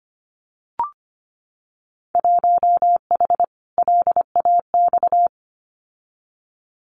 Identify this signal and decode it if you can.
Morse code, heard as a keyed tone, about 25 wpm, 700 Hz. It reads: 1H LAX
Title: Winter Field Day Exchanges 25wpm